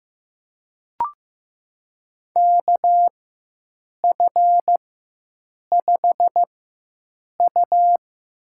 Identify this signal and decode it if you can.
Morse code, heard as a keyed tone, about 15 wpm, spaced as for 7 wpm, 700 Hz. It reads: KF5U
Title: USA Call Signs - 6 Characters - 4x Character Spacing - 15wpm